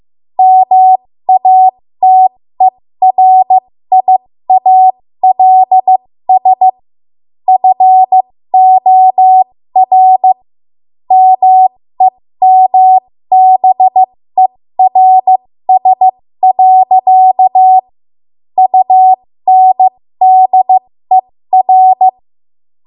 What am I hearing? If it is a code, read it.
Morse code, heard as a keyed tone, about 15 wpm, spaced as for 13 wpm, 750 Hz. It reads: MATERIALS FOR MEMBERS. UNDER
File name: Morse / 13WPM